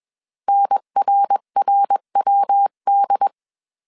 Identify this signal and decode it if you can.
Morse code, heard as a keyed tone, about 21 wpm, 800 Hz. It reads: NRRWD